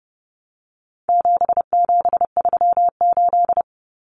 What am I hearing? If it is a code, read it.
Morse code, heard as a keyed tone, about 30 wpm, 700 Hz. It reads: 7738
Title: Numbers - 4-Digit 30wpm